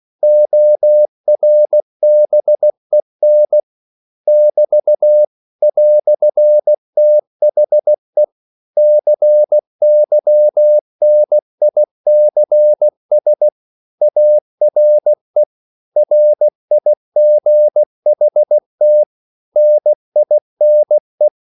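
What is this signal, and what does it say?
Morse code, heard as a keyed tone, about 16 wpm, 600 Hz. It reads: ORBEN <BT> "THE CYNICS ARE RIGHT NINE
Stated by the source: English quotations sent in CW as practice at 16 words per minute.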